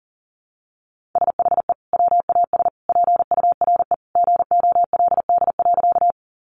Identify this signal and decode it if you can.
Morse code, heard as a keyed tone, about 40 wpm, 700 Hz. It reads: SHE WAS PURE GOLD.
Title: Sentences from Top 1500 Words 40wpm